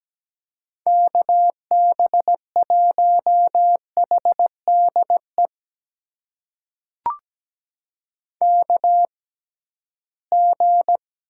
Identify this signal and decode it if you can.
Morse code, heard as a keyed tone, about 17 wpm, spaced as for 13 wpm, 700 Hz. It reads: KB1HDE KG
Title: USA Call Signs - 6 Characters - 6x Character Spacing - 17wpm